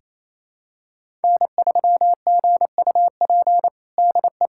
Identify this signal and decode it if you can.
Morse code, heard as a keyed tone, about 27 wpm, 700 Hz. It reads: N3GUP DE
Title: USA Callsign DE Callsign 28wpm